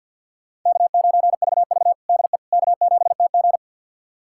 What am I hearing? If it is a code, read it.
Morse code, heard as a keyed tone, about 47 wpm, 700 Hz. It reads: K9VV DE K7TG
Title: USA Callsign DE Callsign - 4 Characters - 50wpm